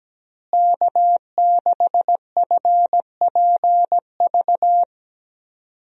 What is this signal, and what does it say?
Morse code, heard as a keyed tone, about 17 wpm, 700 Hz. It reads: K6FPV
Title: USA Call Signs - 5 Characters - Repeated 3x - 17wpm